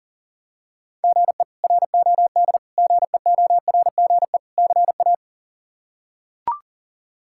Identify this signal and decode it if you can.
Morse code, heard as a keyed tone, about 37 wpm, 700 Hz. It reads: GE ROD GEORGE CA
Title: K1USN SST QSO Practice - 40wpm